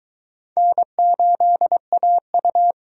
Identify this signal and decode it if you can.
Morse code, heard as a keyed tone, about 22 wpm, 700 Hz. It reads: N8AU